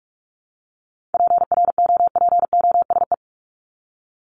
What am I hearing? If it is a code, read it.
Morse code, heard as a keyed tone, about 44 wpm, 700 Hz. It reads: PROPOSE
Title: ED Suffix - Top 100 Words with Base Words 45wpm